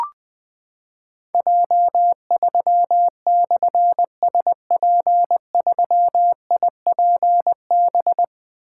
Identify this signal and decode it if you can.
Morse code, heard as a keyed tone, about 20 wpm, 700 Hz. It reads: J3/SP3IPB
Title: Difficult Call Signs 20wpm